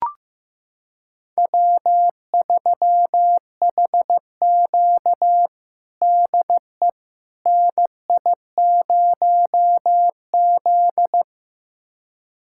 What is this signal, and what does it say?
Morse code, heard as a keyed tone, about 15 wpm, 700 Hz. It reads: W3HQ DE NI0Z
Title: USA Callsign DE Callsign - 4 Characters - 15wpm